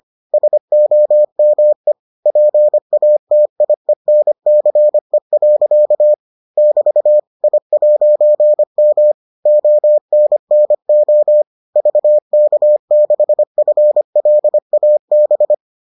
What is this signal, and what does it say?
Morse code, heard as a keyed tone, about 25 wpm, 600 Hz. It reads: SOME PATIENCE. <BT> I'M ONNO VK6FLAB